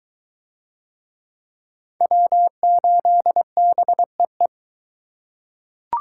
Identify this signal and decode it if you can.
Morse code, heard as a keyed tone, about 21 wpm, 700 Hz. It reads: W8BEE